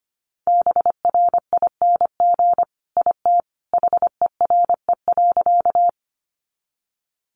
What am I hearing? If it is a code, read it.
Morse code, heard as a keyed tone, about 25 wpm, 700 Hz. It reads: BRING IT HERE.
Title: Sentences from Top 500 Words 25wpm